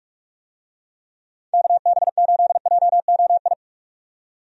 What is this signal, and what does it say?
Morse code, heard as a keyed tone, about 46 wpm, 700 Hz. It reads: KB8JOI
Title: USA Call Signs - 6 Characters - 45wpm